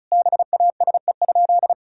Morse code, heard as a keyed tone, about 36 wpm, 690 Hz. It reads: BASE?
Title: Sentences from Top 1000 Words - Review Part 1 - 35wpm